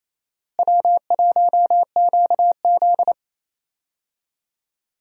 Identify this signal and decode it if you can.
Morse code, heard as a keyed tone, about 28 wpm, 700 Hz. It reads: W1QZ
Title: USA Call Signs - 4 Characters - Repeated 2x - 28wpm